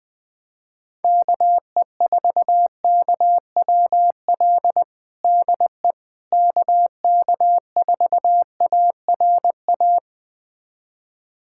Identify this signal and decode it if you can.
Morse code, heard as a keyed tone, about 20 wpm, 700 Hz. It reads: KE4KWL DE KK4ARA
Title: USA Callsign DE Callsign - 6 Characters - 20wpm